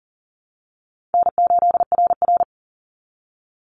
Morse code, heard as a keyed tone, about 37 wpm, 700 Hz. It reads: N8RR
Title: Call Signs Repeated 2x - 40wpm